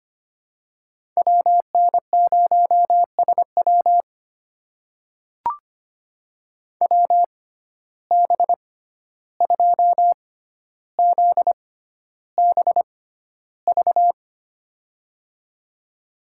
Morse code, heard as a keyed tone, about 25 wpm, spaced as for 12 wpm, 700 Hz. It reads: WN0SW WB2ZBV
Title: USA Call Signs - 6x Character Spacing - 25wpm